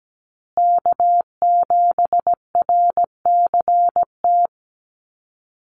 Morse code, heard as a keyed tone, about 17 wpm, 700 Hz. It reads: K7RCT
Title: USA Call Signs Repeated 3x 17wpm